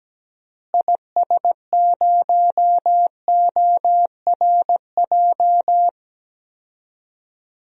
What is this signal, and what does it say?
Morse code, heard as a keyed tone, about 17 wpm, 700 Hz. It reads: IS0ORJ
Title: DXCC-100 Call Signs - 6 Characters - Repeated 2x - 17wpm